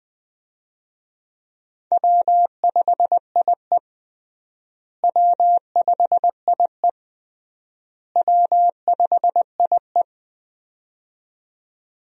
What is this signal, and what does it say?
Morse code, heard as a keyed tone, about 21 wpm, 690 Hz. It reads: W5IE W5IE W5IE
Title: USA Call Signs - 4 Characters - Repeated 3x - 20wpm